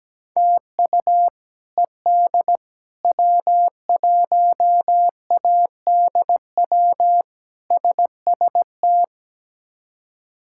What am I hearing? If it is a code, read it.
Morse code, heard as a keyed tone, about 17 wpm, 700 Hz. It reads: TU ED W1ADW SST
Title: K1USN SST QSO Practice - 17wpm